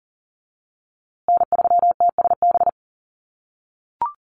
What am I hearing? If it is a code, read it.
Morse code, heard as a keyed tone, about 40 wpm, 700 Hz. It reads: N3TSB
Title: USA Call Signs Repeated 2x 40wpm